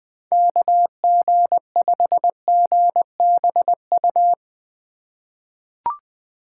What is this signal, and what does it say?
Morse code, heard as a keyed tone, about 21 wpm, 700 Hz. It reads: KG5GBU
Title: USA Call Signs - 6 Characters - 6x Character Spacing - 20wpm